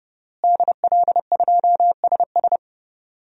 Morse code, heard as a keyed tone, about 32 wpm, 700 Hz. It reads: DL2SS
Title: DXCC-100 Call Signs - 5 Characters - Repeated 3x - 30wpm